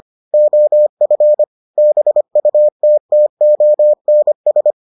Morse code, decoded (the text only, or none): OF BUTTONS